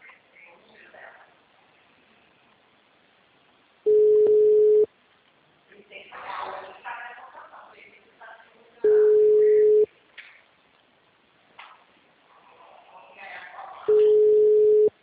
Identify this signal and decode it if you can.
Ringback tone (Europe, 425 Hz), with speech.
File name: out- -2023- - - .36056